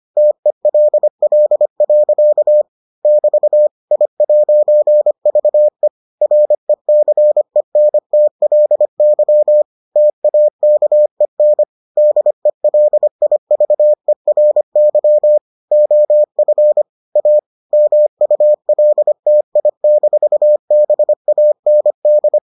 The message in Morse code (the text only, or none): TELL. <BT> I'VE RECENTLY TAKEN DELIVERY OF A MULTI-BAND